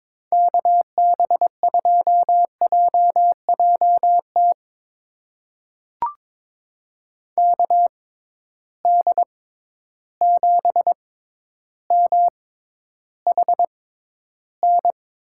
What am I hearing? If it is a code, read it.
Morse code, heard as a keyed tone, about 22 wpm, spaced as for 11 wpm, 700 Hz. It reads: KB2JJT KD7MHN
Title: USA Call Signs - 6 Characters - 6x Character Spacing - 22wpm